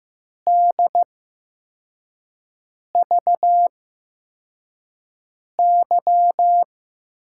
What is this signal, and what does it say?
Morse code, heard as a keyed tone, about 15 wpm, spaced as for 4 wpm, 700 Hz. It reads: DVY
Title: USA Call Signs - 5 Characters - 8x Character Spacing - 15wpm